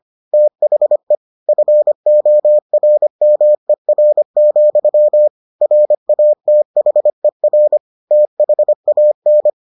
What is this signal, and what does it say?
Morse code, heard as a keyed tone, about 25 wpm, 600 Hz. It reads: THE FORMER, RATHER THAN